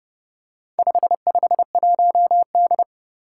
Morse code, heard as a keyed tone, about 30 wpm, 700 Hz. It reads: 551D
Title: Four Challenging Letter-Number Combinations 30wpm